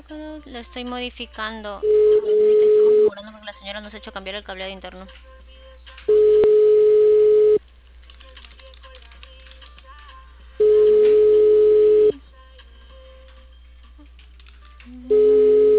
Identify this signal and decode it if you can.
Ringback tone (425 Hz, 1.5 s on, 3 s off), with speech.